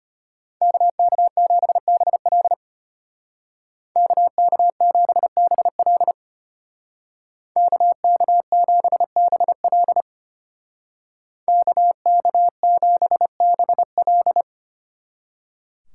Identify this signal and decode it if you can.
Morse code, heard as a keyed tone, about 33 wpm, 700 Hz. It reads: KK7BL KK7BL KK7BL KK7BL